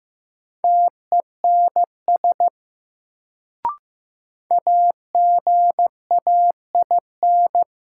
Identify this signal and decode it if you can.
Morse code, heard as a keyed tone, about 15 wpm, 700 Hz. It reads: TENS AGAIN
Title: Sets of 4 Words - New Word Drill - 15wpm